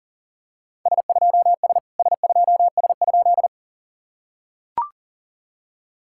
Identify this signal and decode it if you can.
Morse code, heard as a keyed tone, about 44 wpm, 690 Hz. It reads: S2S S2S?
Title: SOTA QSO Practice 40wpm